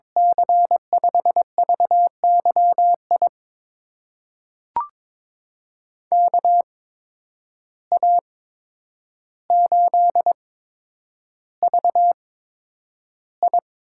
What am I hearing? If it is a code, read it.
Morse code, heard as a keyed tone, about 22 wpm, spaced as for 9 wpm, 700 Hz. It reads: C5VYI KA8VI
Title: USA Call Signs - 6 Characters - 8x Character Spacing - 22wpm